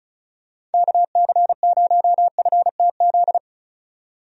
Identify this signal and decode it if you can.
Morse code, heard as a keyed tone, about 36 wpm, 700 Hz. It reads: KC0FTZ
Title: USA Call Signs - 6 Characters - 35wpm